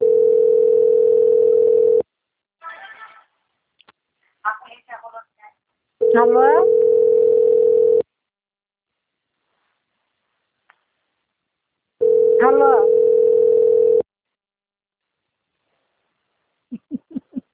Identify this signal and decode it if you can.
Ringback tone (North America), 440 + 480 Hz, with speech.
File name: out- - - - - .59286